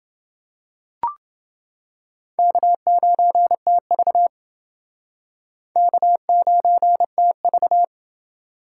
Morse code, heard as a keyed tone, about 28 wpm, 700 Hz. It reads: K9TV K9TV